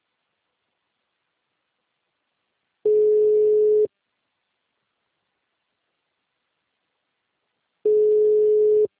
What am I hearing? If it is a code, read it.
Ringback tone (Europe, 425 Hz).
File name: out- -2030- - - .87